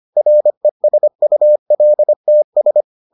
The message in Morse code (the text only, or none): RESULTS